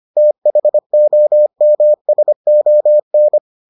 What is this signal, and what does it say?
Morse code, heard as a keyed tone, about 25 wpm, 600 Hz. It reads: THOMSON